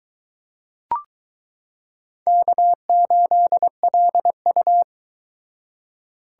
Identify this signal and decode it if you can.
Morse code, heard as a keyed tone, about 22 wpm, 700 Hz. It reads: K8LU